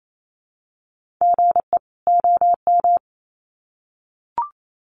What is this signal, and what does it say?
Morse code, heard as a keyed tone, about 28 wpm, 700 Hz. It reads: GE OM
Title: Common QSO Phrases 28wpm